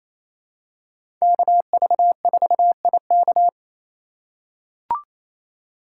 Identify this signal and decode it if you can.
Morse code, heard as a keyed tone, about 28 wpm, 700 Hz. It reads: KV4IK
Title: USA Call Signs - 5 Characters - 8x Character Spacing - 28wpm